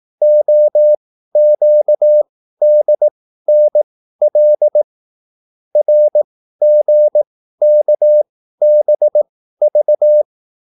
Morse code, heard as a keyed tone, about 18 wpm, spaced as for 13 wpm, 600 Hz. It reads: OQDNL RGKBV